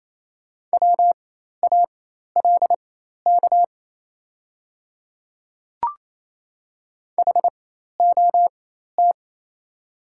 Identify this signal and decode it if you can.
Morse code, heard as a keyed tone, about 28 wpm, spaced as for 13 wpm, 700 Hz. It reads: WALK HOT
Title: 4x Character Spacing - Top 300 Words - 28wpm